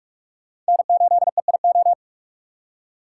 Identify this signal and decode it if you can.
Morse code, heard as a keyed tone, about 56 wpm, 700 Hz. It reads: N8EIO
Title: Speed Racing - USA 1x3 Call Signs 45wpm